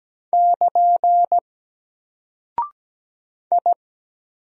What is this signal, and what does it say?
Morse code, heard as a keyed tone, about 17 wpm, 700 Hz. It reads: <KN> I
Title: Single Character (Letter:Number:Punctuation:Prosign) Alt 17wpm